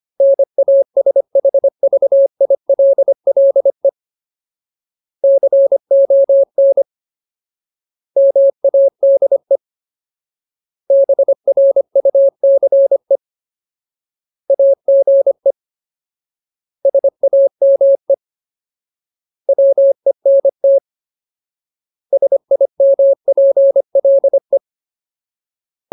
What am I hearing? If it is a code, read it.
Morse code, heard as a keyed tone, about 25 wpm, 550 Hz. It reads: NASHVILLE CON MADE BRUCE AGE SAME WENT SIMPLE
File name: Ultimate Word List / 25WPM Ultimate